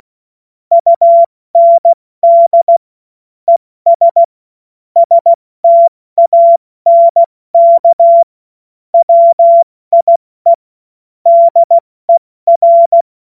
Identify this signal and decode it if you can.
Morse code, heard as a keyed tone, about 16 wpm, 680 Hz. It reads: UND ES STANK WIE DER
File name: Morsekurs / Witze / Witz 07. März 2023 70 F80